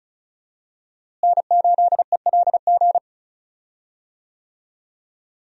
Morse code, heard as a keyed tone, about 36 wpm, 700 Hz. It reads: N8ELG
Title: USA Call Signs - 5 Characters - Repeated 3x - 35wpm